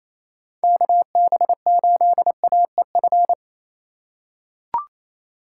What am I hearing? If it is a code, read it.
Morse code, heard as a keyed tone, about 28 wpm, 700 Hz. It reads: KB8AEF
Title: USA Call Signs - 6 Characters - 4x Character Spacing - 28wpm